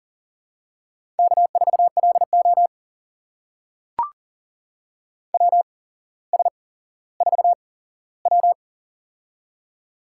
Morse code, heard as a keyed tone, about 38 wpm, spaced as for 17 wpm, 700 Hz. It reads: K4LO WS4W
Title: USA Call Signs - 4 Characters - 8x Character Spacing - 40wpm